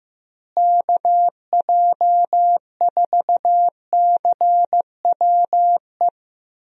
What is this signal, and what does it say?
Morse code, heard as a keyed tone, about 15 wpm, 700 Hz. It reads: KJ4CWE
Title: USA Call Signs - 6 Characters - Repeated 2x - 15wpm